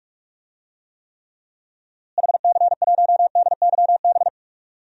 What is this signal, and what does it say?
Morse code, heard as a keyed tone, about 42 wpm, 700 Hz. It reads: HC1DYB
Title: DXCC-100 Call Signs - 6 Characters - Repeated 2x - 45wpm